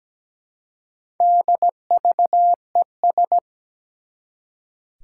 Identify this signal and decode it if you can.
Morse code, heard as a keyed tone, about 17 wpm, 700 Hz. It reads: DVES
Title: Four Challenging Letter-Number Combinations 17wpm